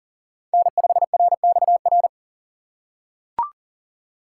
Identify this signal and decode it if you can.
Morse code, heard as a keyed tone, about 37 wpm, 700 Hz. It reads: N5RXR
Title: USA Call Signs - 5 Characters - 2x Character Spacing - 40wpm